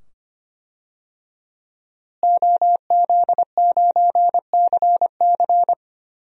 Morse code, heard as a keyed tone, about 25 wpm, 700 Hz. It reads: OZ9CC
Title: DXCC-100 Call Signs - 5 Characters - Repeated 2x - 25wpm